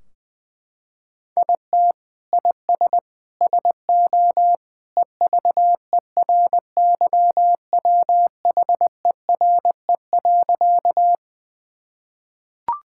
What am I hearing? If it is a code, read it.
Morse code, heard as a keyed tone, about 20 wpm, 700 Hz. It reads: IT IS SO EVERYWHERE.
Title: Sentences from Top 1600 Words 20wpm